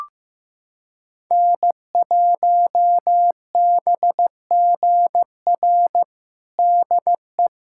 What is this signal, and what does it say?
Morse code, heard as a keyed tone, about 15 wpm, 700 Hz. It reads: N1BGR DE
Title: USA Callsign DE Callsign 15wpm